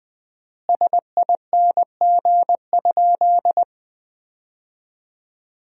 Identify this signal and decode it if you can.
Morse code, heard as a keyed tone, about 21 wpm, 700 Hz. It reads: SING?
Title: 4x Word-Spacing - Sentences from Top 400 Words - 20wpm